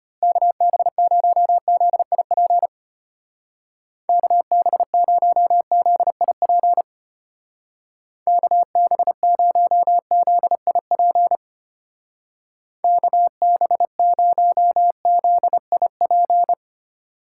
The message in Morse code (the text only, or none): KB0ZIP KB0ZIP KB0ZIP KB0ZIP